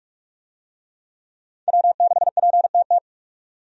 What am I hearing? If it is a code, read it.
Morse code, heard as a keyed tone, about 46 wpm, 700 Hz. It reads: W6PTT
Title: USA Call Signs - 5 Characters - 45wpm